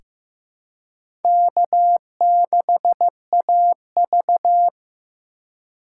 Morse code, heard as a keyed tone, about 16 wpm, 700 Hz. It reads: K6AV
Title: USA Call Signs - 4 Characters - 4x Character Spacing - 15wpm